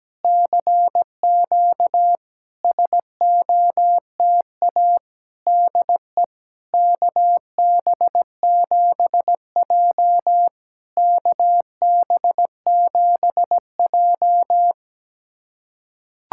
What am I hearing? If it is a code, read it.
Morse code, heard as a keyed tone, about 17 wpm, 700 Hz. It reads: CQ SOTA DE KB7J KB7J
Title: SOTA QSO Practice 17wpm